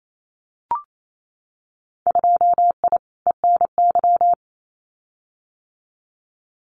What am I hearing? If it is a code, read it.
Morse code, heard as a keyed tone, about 28 wpm, 700 Hz. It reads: 2I ENY
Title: Winter Field Day Exchanges Encore 28wpm